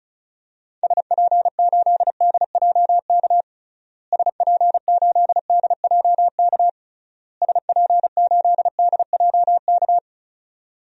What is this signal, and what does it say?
Morse code, heard as a keyed tone, about 36 wpm, 700 Hz. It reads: SP8DJK SP8DJK SP8DJK
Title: DXCC-100 Call Signs - 6 Characters - Repeated 3x - 35wpm